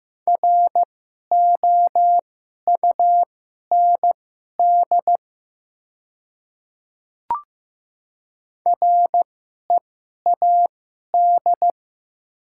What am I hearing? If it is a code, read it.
Morse code, heard as a keyed tone, about 15 wpm, spaced as for 11 wpm, 700 Hz. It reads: ROUND READ
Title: 2x Character Spacing - Top 200 Words - 15wpm